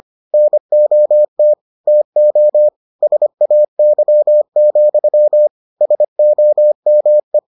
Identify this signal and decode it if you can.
Morse code, heard as a keyed tone, about 25 wpm, 600 Hz. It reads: NOT TO SAY, SOME